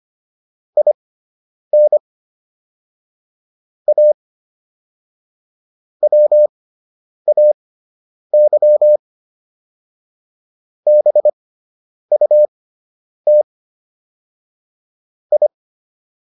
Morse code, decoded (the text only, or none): IN A WAY BUT I